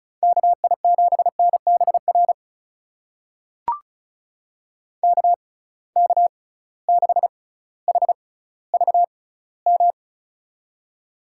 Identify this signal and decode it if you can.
Morse code, heard as a keyed tone, about 36 wpm, spaced as for 18 wpm, 700 Hz. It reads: KI7NBR KK6HVM
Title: USA Call Signs - 6 Characters - 6x Character Spacing - 35wpm